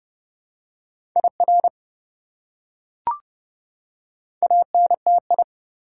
Morse code, heard as a keyed tone, about 32 wpm, 700 Hz. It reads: IR ANTI
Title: Prefixes and Suffixes 30wpm